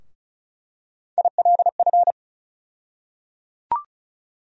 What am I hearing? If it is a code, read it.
Morse code, heard as a keyed tone, about 36 wpm, 700 Hz. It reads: ILF
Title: USA Call Sign Postfixes - 3 Letters 35wpm